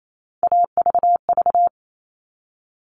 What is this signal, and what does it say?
Morse code, heard as a keyed tone, about 28 wpm, 700 Hz. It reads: AVV
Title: All Cut Numbers - 3-Digit 28wpm